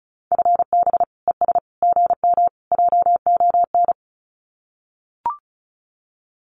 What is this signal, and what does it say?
Morse code, heard as a keyed tone, about 34 wpm, 700 Hz. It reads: FB ES GM JON
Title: Common QSO Phrases Encore 35wpm